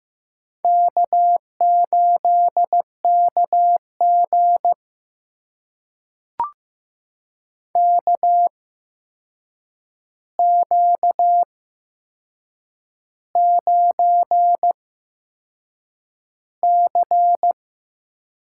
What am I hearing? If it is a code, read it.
Morse code, heard as a keyed tone, about 15 wpm, spaced as for 6 wpm, 700 Hz. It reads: K8KG KQ9C
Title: USA Call Signs - 4 Characters - 8x Character Spacing - 15wpm